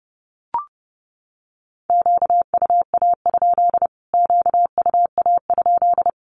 Morse code, heard as a keyed tone, about 30 wpm, 700 Hz. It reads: QUA? QUA?
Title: Q-Codes Repeated 3x 30wpm